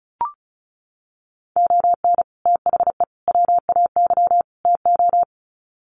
Morse code, heard as a keyed tone, about 34 wpm, 700 Hz. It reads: ON THE WAY TO
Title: Sets of 4 Words 35wpm